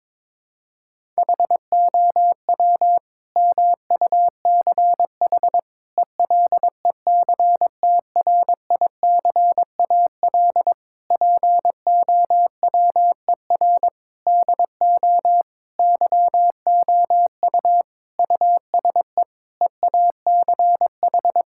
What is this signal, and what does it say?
Morse code, heard as a keyed tone, about 22 wpm, 700 Hz. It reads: HOW MUCH ELECTRICAL POWER DO YOU USE EACH
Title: Sentences from Top 1000 Words - Review Part 6 - 22wpm